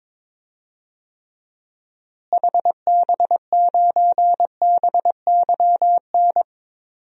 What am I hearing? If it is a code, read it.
Morse code, heard as a keyed tone, about 22 wpm, 700 Hz. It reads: HB9BYN
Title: DXCC-100 Call Signs - 6 Characters - Repeated 2x - 22wpm